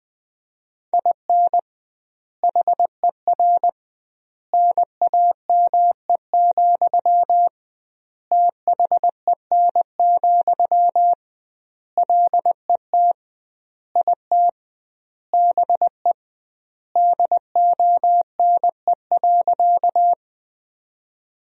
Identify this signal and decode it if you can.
Morse code, heard as a keyed tone, about 20 wpm, 700 Hz. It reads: IN HER NAME, THEN, LET IT BE DONE.
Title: 2x Word-Spacing - Sentences from Top 500 Words - 20wpm